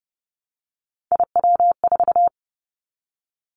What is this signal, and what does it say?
Morse code, heard as a keyed tone, about 30 wpm, 690 Hz. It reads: IW4
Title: Callsign Prefixes for Top 25 DXCC - Repeated 3x - 30wpm